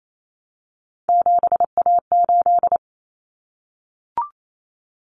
Morse code, heard as a keyed tone, about 28 wpm, 700 Hz. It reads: 7A8
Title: Common Cut Numbers - 3-Digit 28wpm